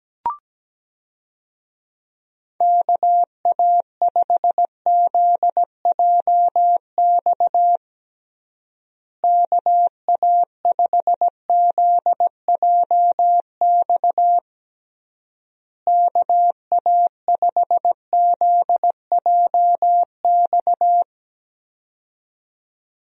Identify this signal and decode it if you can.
Morse code, heard as a keyed tone, about 17 wpm, 700 Hz. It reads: KA5ZJX KA5ZJX KA5ZJX
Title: USA Call Signs - 6 Characters - Repeated 3x - 17wpm